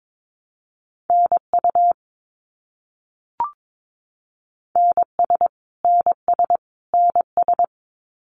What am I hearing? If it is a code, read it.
Morse code, heard as a keyed tone, about 22 wpm, 700 Hz. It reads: NU NS NS NS
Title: Canadian Provinces and Territories - Repeated 3x - 22wpm